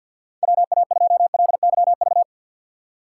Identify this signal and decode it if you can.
Morse code, heard as a keyed tone, about 50 wpm, 700 Hz. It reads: WA2LYV
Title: USA Call Signs - 6 Characters - 4x Character Spacing - 50wpm